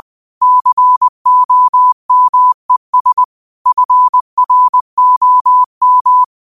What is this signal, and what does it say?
Morse code, heard as a keyed tone, about 20 wpm, 1000 Hz. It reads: COMES FROM
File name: qotdfiles / 20wpm